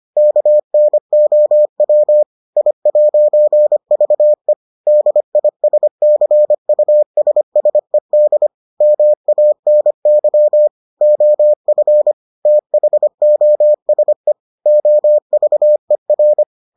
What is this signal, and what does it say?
Morse code, heard as a keyed tone, about 25 wpm, 600 Hz. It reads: KNOW I'VE DISCUSSED MANY OF THOSE OVER